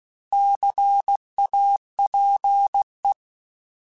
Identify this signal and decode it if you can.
Morse code, heard as a keyed tone, about 16 wpm, 800 Hz. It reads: CAPE
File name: Alphabetical / Alphabetical mp3s / Cs @15